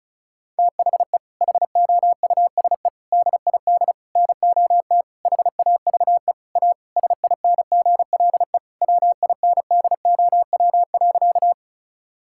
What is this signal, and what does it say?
Morse code, heard as a keyed tone, about 36 wpm, 700 Hz. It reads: THE HOUSE DID NOT HAVE A SINGLE WINDOW.
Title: Sentences from Top 1600 Words 35wpm